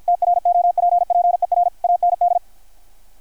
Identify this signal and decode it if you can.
Morse code, heard as a keyed tone, about 53 wpm, 700 Hz. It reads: TROPPER AND
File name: archived / CRSnet